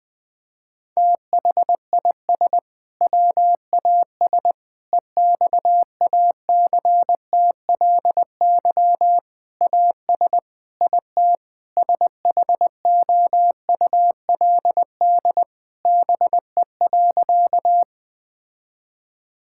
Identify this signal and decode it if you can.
Morse code, heard as a keyed tone, about 20 wpm, 700 Hz. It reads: THIS WAS EXACTLY AS IT SHOULD BE.
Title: Sentences from Top 1000 Words - Review Part 6 - 20wpm